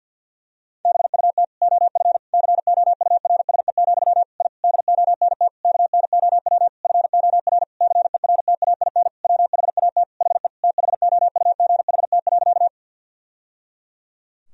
Morse code, heard as a keyed tone, about 49 wpm, 700 Hz. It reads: BUT OF COURSE, I DONT KNOW FOR CERTAIN WHAT HE THOUGHT.
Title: Sentences from Top 1000 Words - Review Part 2 - 50wpm